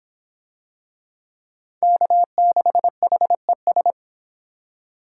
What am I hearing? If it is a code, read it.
Morse code, heard as a keyed tone, about 23 wpm, 700 Hz. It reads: K6HES